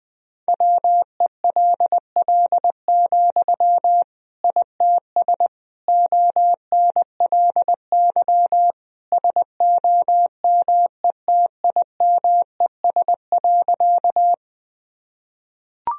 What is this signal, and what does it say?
Morse code, heard as a keyed tone, about 21 wpm, 700 Hz. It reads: WELL, ITS ONLY SOMETIMES.
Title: Sentences from Top 1000 Words - Review Part 3 - 20wpm